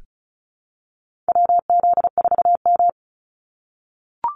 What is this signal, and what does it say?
Morse code, heard as a keyed tone, about 34 wpm, 700 Hz. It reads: WZ4M
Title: NA SOTA Activator Callsigns 35wpm